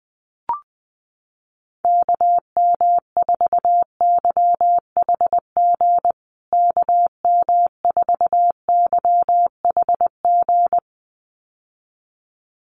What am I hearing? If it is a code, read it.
Morse code, heard as a keyed tone, about 20 wpm, 700 Hz. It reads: KM4YHG KM4YHG
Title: USA Call Signs Repeated 2x 20wpm